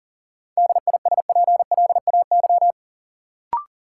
Morse code, heard as a keyed tone, about 40 wpm, 700 Hz. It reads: DISPLAY
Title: Sentences from Top 1200 Words - New Word Drill Alt - 40wpm